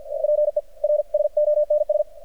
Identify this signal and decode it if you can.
Morse code, heard as a keyed tone, about 53 wpm, spaced as for 42 wpm, 600 Hz, with radio noise. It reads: PERSONS
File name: archived / CRSnet